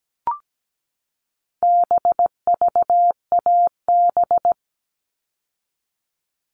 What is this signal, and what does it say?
Morse code, heard as a keyed tone, about 17 wpm, 700 Hz. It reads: BVAB
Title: All Cut Numbers - 4-Digit 17wpm